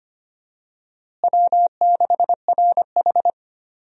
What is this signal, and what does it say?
Morse code, heard as a keyed tone, about 25 wpm, 700 Hz. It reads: W6RH
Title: USA Call Signs - 4 Characters - Repeated 3x - 25wpm